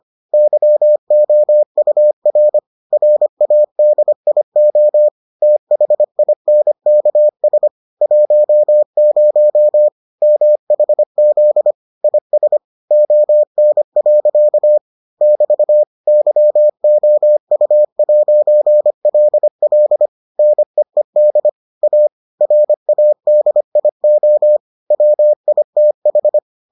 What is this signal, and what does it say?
Morse code, heard as a keyed tone, about 25 wpm, 600 Hz. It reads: YOUR RADIO THINKS 10 MHZ IS ON. <BT> YOU'LL NEED A RADIO WITH